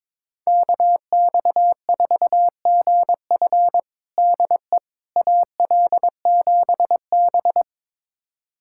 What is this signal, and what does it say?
Morse code, heard as a keyed tone, about 22 wpm, 700 Hz. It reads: KX4GF DE AL7B
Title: USA Callsign DE Callsign 22wpm